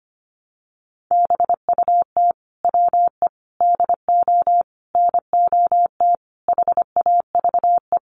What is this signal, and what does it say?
Morse code, heard as a keyed tone, about 25 wpm, 700 Hz. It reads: BUT WE DO NOT HAVE
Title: Sets of 5 Words Encore2 25wpm